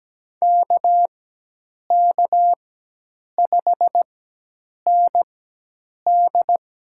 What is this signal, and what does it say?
Morse code, heard as a keyed tone, about 17 wpm, spaced as for 8 wpm, 700 Hz. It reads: KK5ND
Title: USA Call Signs - 5 Characters - 4x Character Spacing - 17wpm